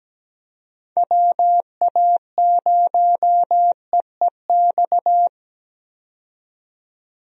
Morse code, heard as a keyed tone, about 17 wpm, 700 Hz. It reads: WA0EEX
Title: USA Call Signs - 6 Characters - 17wpm